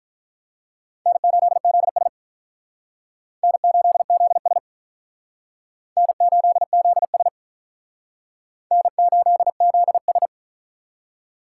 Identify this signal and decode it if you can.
Morse code, heard as a keyed tone, about 46 wpm, 700 Hz. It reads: N8ZS N8ZS N8ZS N8ZS